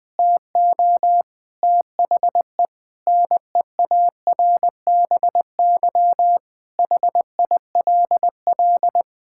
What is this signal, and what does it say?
Morse code, heard as a keyed tone, about 20 wpm, 700 Hz. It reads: TO THE NEARBY HILL
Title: Difficult Sentences with Hints - 20wpm